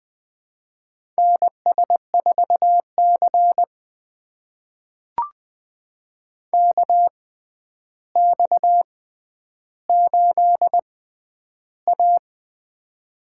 Morse code, heard as a keyed tone, about 21 wpm, spaced as for 10 wpm, 700 Hz. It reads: NS4C KX8A
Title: USA Call Signs - 4 Characters - 6x Character Spacing - 20wpm